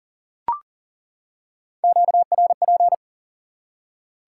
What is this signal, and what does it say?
Morse code, heard as a keyed tone, about 37 wpm, 700 Hz. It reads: QRP
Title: Top 180 QSO Elements 40wpm